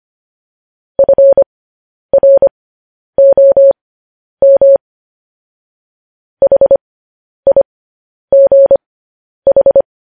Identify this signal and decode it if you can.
Morse code, heard as a keyed tone, about 25 wpm, spaced as for 10 wpm, 550 Hz. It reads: FROM HIGH
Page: news-headlines-in-morse-code-at-10-wpm